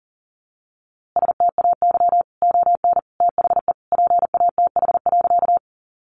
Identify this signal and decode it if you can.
Morse code, heard as a keyed tone, about 40 wpm, 700 Hz. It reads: STAY ON THE PATH.
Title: A Sound Of Thunder 40wpm